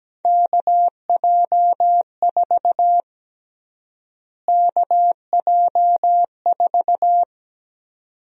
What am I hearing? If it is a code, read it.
Morse code, heard as a keyed tone, about 17 wpm, 700 Hz. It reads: KJ4 KJ4
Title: USA Call Sign Prefixes - Repeated 2x - 17wpm